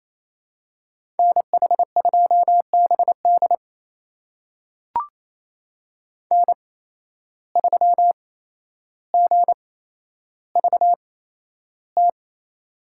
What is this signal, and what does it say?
Morse code, heard as a keyed tone, about 28 wpm, spaced as for 12 wpm, 700 Hz. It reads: NH2BD N3GVT
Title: USA Call Signs - 5 Characters - 8x Character Spacing - 28wpm